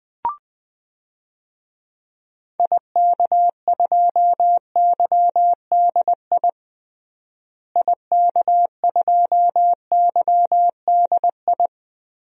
Morse code, heard as a keyed tone, about 20 wpm, 700 Hz. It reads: IK2YDI IK2YDI
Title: DXCC-100 Call Signs - 6 Characters - Repeated 3x - 20wpm